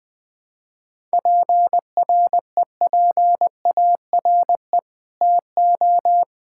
Morse code, heard as a keyed tone, about 20 wpm, 700 Hz. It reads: PREPARE TO
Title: Sentences from Top 1000 Words - Review Part 6 - 20wpm